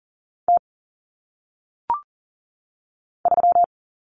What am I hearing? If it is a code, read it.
Morse code, heard as a keyed tone, about 40 wpm, 700 Hz. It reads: T 3
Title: Common Cut Numbers - 1-Digit 40wpm